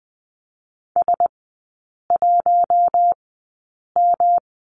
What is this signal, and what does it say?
Morse code, heard as a keyed tone, about 20 wpm, spaced as for 8 wpm, 700 Hz. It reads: S1M
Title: 133 Lesson 22e - Question Mark - Three Characters 20wpm